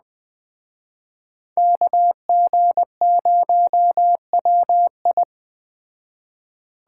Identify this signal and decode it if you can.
Morse code, heard as a keyed tone, about 20 wpm, 700 Hz. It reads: KG0WI